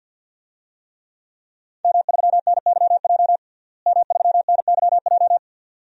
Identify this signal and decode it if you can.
Morse code, heard as a keyed tone, about 49 wpm, 700 Hz. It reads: M3NYJ M3NYJ
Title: DXCC-100 Call Signs - 5 Characters - Repeated 2x - 50wpm